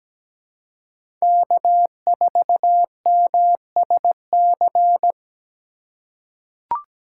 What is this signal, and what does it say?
Morse code, heard as a keyed tone, about 17 wpm, 700 Hz. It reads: K4MSC
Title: USA Call Signs - 5 Characters - 6x Character Spacing - 17wpm